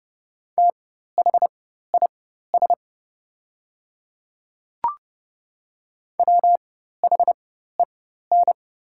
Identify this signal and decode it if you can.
Morse code, heard as a keyed tone, about 30 wpm, spaced as for 14 wpm, 700 Hz. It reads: THIS WHEN
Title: 4x Character Spacing - Top 100 Words - 30wpm